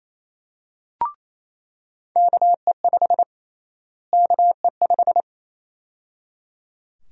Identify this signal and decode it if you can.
Morse code, heard as a keyed tone, about 28 wpm, 700 Hz. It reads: KE5 KE5
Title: USA Call Sign Prefixes - Repeated 2x - 28wpm